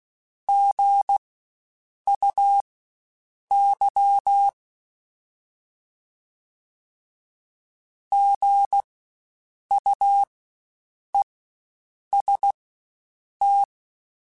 Morse code, heard as a keyed tone, about 16 wpm, spaced as for 7 wpm, 800 Hz. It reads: GUY GUEST
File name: Spaced Alphas / Spaced Alphas mp3s / Gs sp @15